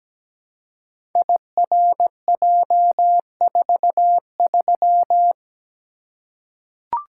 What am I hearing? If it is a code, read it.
Morse code, heard as a keyed tone, about 17 wpm, 700 Hz. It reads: IRJ43
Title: Code Groups - 5 Letter:Numbers - 17wpm